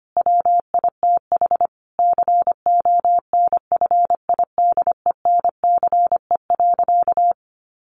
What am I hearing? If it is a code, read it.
Morse code, heard as a keyed tone, about 25 wpm, 700 Hz. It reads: WITH CONFIDENCE.
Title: Sentences from Top 1500 Words 25wpm